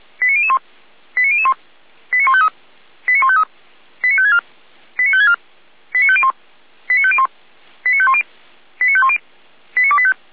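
Five-tone selective call (ZVEI), heard as quick runs of five tones, tones of about 70 ms. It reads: 89001 89001 89153 89153 89564 89564 89501 89501 89510 89510 89285
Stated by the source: ZVEI five-tone selective call, address 89150, received on an SDR.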